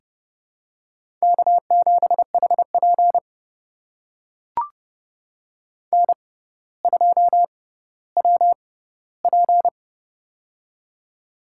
Morse code, heard as a keyed tone, about 30 wpm, spaced as for 15 wpm, 700 Hz. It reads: K7HP N2WP
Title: USA Call Signs - 4 Characters - 6x Character Spacing - 30wpm